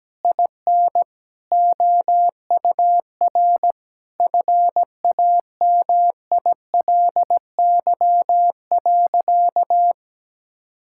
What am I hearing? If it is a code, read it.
Morse code, heard as a keyed tone, about 17 wpm, 700 Hz. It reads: IN OUR FAMILY.
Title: Sentences from Top 1000 Words - Review Part 6 - 17wpm